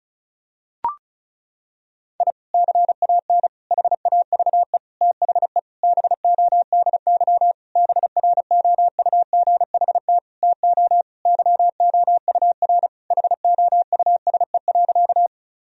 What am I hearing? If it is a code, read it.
Morse code, heard as a keyed tone, about 34 wpm, 700 Hz. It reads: I CAN HAVE THE BODY BROUGHT TO YOUR HOUSE.
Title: Sentences from Top 1000 Words - Review Part 2 - 35wpm